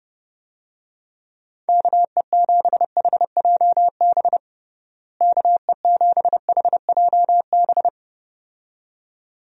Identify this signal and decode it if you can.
Morse code, heard as a keyed tone, about 30 wpm, 700 Hz. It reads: KE7HJB KE7HJB
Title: USA Call Signs - 6 Characters - Repeated 2x - 30wpm